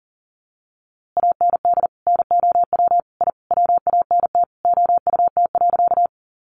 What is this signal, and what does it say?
Morse code, heard as a keyed tone, about 40 wpm, 700 Hz. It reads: AND NOW I WANT OUT.
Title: A Sound Of Thunder 40wpm